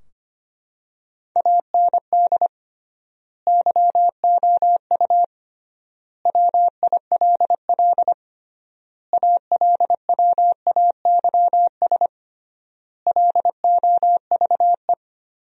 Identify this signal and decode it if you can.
Morse code, heard as a keyed tone, about 25 wpm, 700 Hz. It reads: AND YOU WILL ALWAYS LOVE
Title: 3x Word-Spacing - Sentences from Top 400 Words - 25wpm